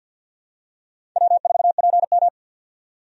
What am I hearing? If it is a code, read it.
Morse code, heard as a keyed tone, about 51 wpm, 700 Hz. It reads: W4PM
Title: USA Call Signs - 4 Characters - 50wpm